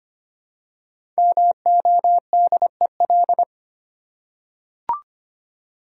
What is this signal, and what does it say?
Morse code, heard as a keyed tone, about 25 wpm, 700 Hz. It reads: MODEL
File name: 5 Letter Words - First 3 Letters Given - 25wpm